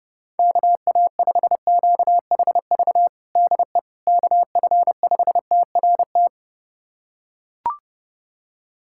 Morse code, heard as a keyed tone, about 30 wpm, 700 Hz. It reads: KA5QHV DE KF5TRT
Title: USA Callsign DE Callsign - 6 Characters - 30wpm